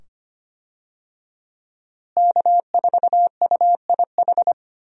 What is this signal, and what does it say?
Morse code, heard as a keyed tone, about 25 wpm, 700 Hz. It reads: K4UIH
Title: USA Call Signs - 5 Characters - Repeated 2x - 25wpm